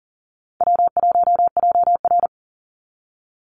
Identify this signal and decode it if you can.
Morse code, heard as a keyed tone, about 40 wpm, 700 Hz. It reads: W1JR
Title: Easy Call Signs Repeated 2x - 40wpm